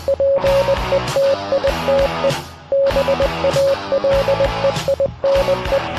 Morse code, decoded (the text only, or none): PETITE BALEINE